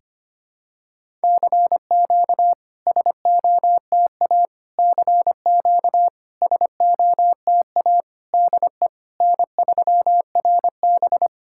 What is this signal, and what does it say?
Morse code, heard as a keyed tone, about 25 wpm, 700 Hz. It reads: CQ SOTA CQ SOTA DE N3RB
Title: SOTA QSO Practice 25wpm